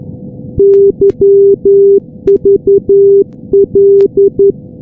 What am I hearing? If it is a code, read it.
Morse code, heard as a keyed tone, about 11 wpm, 400 Hz. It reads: YVL
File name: europe / g-443-yvl-343-sdr-hu-g-south west uk- -1245z